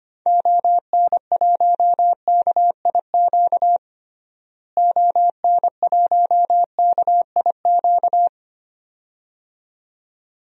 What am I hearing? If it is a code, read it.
Morse code, heard as a keyed tone, about 25 wpm, 700 Hz. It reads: ON1KIQ ON1KIQ
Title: DXCC-100 Call Signs - 6 Characters - Repeated 2x - 25wpm